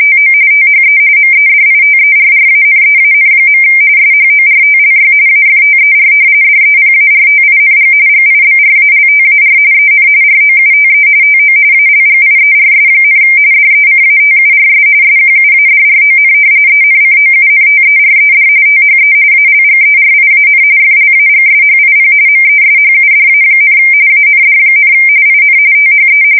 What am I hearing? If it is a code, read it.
RTTY, about 45.45 baud. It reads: TO THE MEDITERRANEAN. SHORTLY AFTER DEPARTING GIBRALTAR EN ROUTE TO ALGIERS, DECATURS SQUADRON ENCOUNTERED THE ALGERIAN FLAGSHIP MESHUDA, AND, IN THE
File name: fsk170 / usnavy2